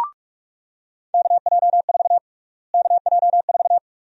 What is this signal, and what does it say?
Morse code, heard as a keyed tone, about 42 wpm, 700 Hz. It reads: KJ4 KJ4
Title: USA Call Sign Prefixes - Repeated 2x - 45wpm